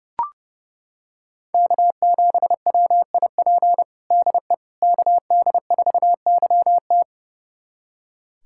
Morse code, heard as a keyed tone, about 30 wpm, 700 Hz. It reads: K7WIP DE KD4YT
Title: USA Callsign DE Callsign - 5 Characters - 30wpm